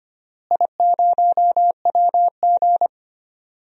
Morse code, heard as a keyed tone, about 25 wpm, 700 Hz. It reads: I0WG
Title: DXCC-100 Call Signs - 4 Characters - Repeated 2x - 25wpm